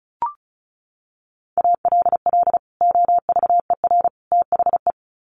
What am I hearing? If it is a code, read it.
Morse code, heard as a keyed tone, about 34 wpm, 700 Hz. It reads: ALL OVER THE
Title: Sets of 3 Words 35wpm